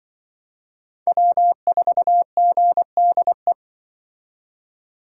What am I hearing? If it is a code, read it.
Morse code, heard as a keyed tone, about 25 wpm, 700 Hz. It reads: W4GDE
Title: Speed Racing - USA 1x3 Call Signs 20wpm